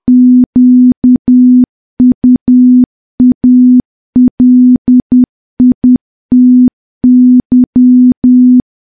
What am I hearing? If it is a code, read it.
Morse code, heard as a keyed tone, about 10 wpm, 250 Hz. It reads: QUALITY
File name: morse / morse-training-10WPM